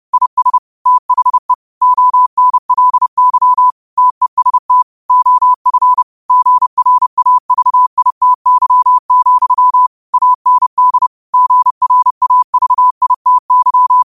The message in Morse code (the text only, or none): IS THE ONLY TEST OF GRAVITY, AND GRAVITY